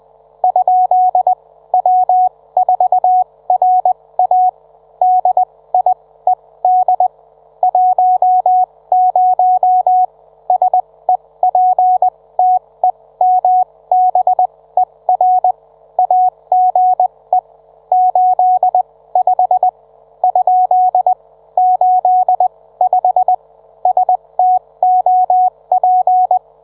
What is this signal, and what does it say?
Morse code, heard as a keyed tone, about 20 wpm, spaced as for 17 wpm, 740 Hz. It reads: ? W4RA DIED 10 SEPTEMBER AGE 85 ? 85 STOP